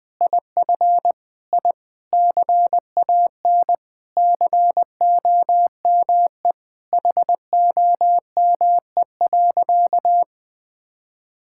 Morse code, heard as a keyed tone, about 20 wpm, 700 Hz. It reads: IF I CAN COME HOME.
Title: Sentences with 5-Word Phrases - Spoken Beginning - 20wpm